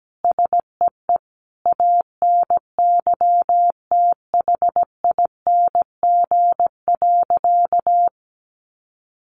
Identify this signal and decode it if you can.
Morse code, heard as a keyed tone, about 17 wpm, 700 Hz. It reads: SEE ANYTHING.
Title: Additional Easy Phrases 17wpm